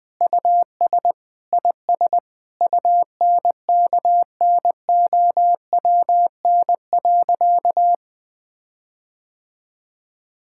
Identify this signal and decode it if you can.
Morse code, heard as a keyed tone, about 20 wpm, 700 Hz. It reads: US IS UNKNOWN.
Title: Sentences with 4-Word Phrases - 20wpm